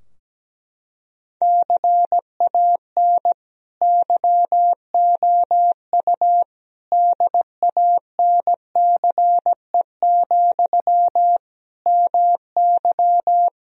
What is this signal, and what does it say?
Morse code, heard as a keyed tone, about 17 wpm, 700 Hz. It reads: CAN YOU DANCE, MY
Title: Sentences from Top 1000 Words - Review Part 4 - 17wpm